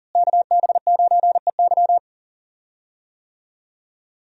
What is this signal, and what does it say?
Morse code, heard as a keyed tone, about 40 wpm, 700 Hz. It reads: KB9EY
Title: USA Call Signs - 5 Characters - Repeated 2x - 40wpm